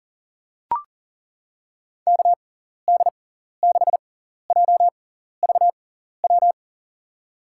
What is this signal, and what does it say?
Morse code, heard as a keyed tone, about 40 wpm, spaced as for 14 wpm, 700 Hz. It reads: KD6JVW
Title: USA Call Signs - 6x Character Spacing - 40wpm